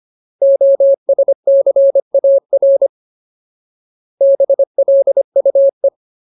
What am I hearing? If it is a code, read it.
Morse code, heard as a keyed tone, about 25 wpm, 550 Hz. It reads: OSCAR BLUE
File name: Ultimate Word List / 25WPM Ultimate